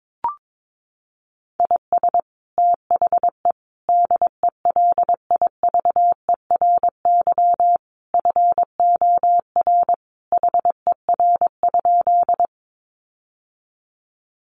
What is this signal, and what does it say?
Morse code, heard as a keyed tone, about 22 wpm, 700 Hz. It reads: IS THE DELIVERY FOR HER?
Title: Sentences from Top 1500 Words 22wpm